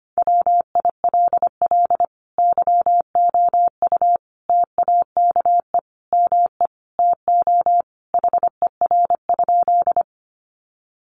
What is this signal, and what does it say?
Morse code, heard as a keyed tone, about 25 wpm, 700 Hz. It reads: WILL YOU TAKE ME TO HER?
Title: Sentences from Top 200 Words 25wpm